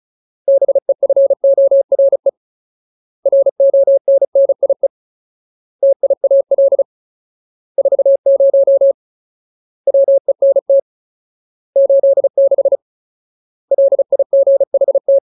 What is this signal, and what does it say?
Morse code, heard as a keyed tone, about 34 wpm, 550 Hz. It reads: BEFORE RONNIE TIAL 40 WENT 86 LIGHT
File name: Ultimate Word List / 35WPM Ultimate